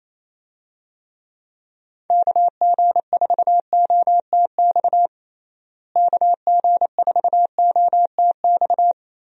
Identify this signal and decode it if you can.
Morse code, heard as a keyed tone, about 28 wpm, 700 Hz. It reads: KG4OTX KG4OTX
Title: USA Call Signs - 6 Characters - Repeated 3x - 28wpm